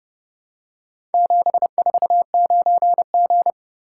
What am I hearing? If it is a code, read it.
Morse code, heard as a keyed tone, about 29 wpm, 700 Hz. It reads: 749G